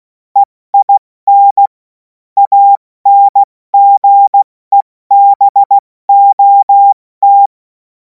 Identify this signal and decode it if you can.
Morse code, heard as a keyed tone, about 16 wpm, 800 Hz. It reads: EIN ANGEBOT
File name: Morsekurs / Witze / Witz 30. Juni 2026 70 F80